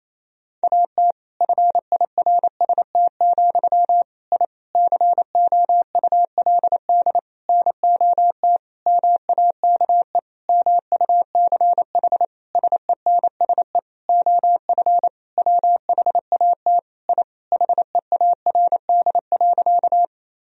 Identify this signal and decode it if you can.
Morse code, heard as a keyed tone, about 28 wpm, 700 Hz. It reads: AT FIRST, I COULD NOT MAKE MUCH SENSE OF WHAT I HEARD.
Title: Sentences from Top 1000 Words - Review Part 4 - 28wpm